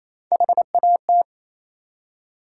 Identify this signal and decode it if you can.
Morse code, heard as a keyed tone, about 28 wpm, 700 Hz. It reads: HAT